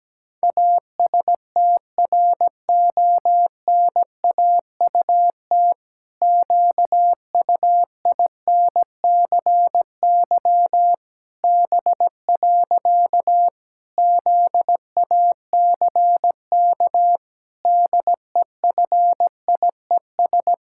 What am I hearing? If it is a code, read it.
Morse code, heard as a keyed tone, about 17 wpm, 700 Hz. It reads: ASTRONAUT QUINCY B. ZACK DEFIES
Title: Pangram Challenge v2 - 17wpm